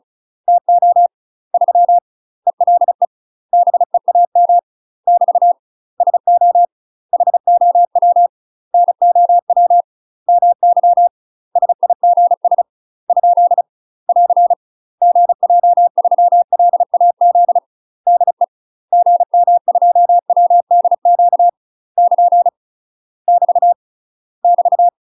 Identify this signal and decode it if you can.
Morse code, heard as a keyed tone, about 36 wpm, 700 Hz. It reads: TO 3 ELE BEAM <BT> SO HOW NOW MY SIGS ? <AR> GJ3LAZ DE GM2WDQ <KN> <BT> <BT>
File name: UK MASTER QSOs TRAINING / 35WPM / UK MASTER QSOs 0016